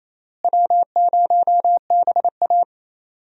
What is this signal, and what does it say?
Morse code, heard as a keyed tone, about 27 wpm, 700 Hz. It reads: W0BA
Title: USA Call Signs - 4 Characters - Repeated 2x - 28wpm